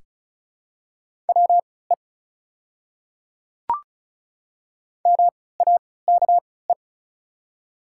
Morse code, heard as a keyed tone, about 35 wpm, spaced as for 20 wpm, 700 Hz. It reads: WE MAKE
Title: 3x Character Spacing - Top 100 Words - 35wpm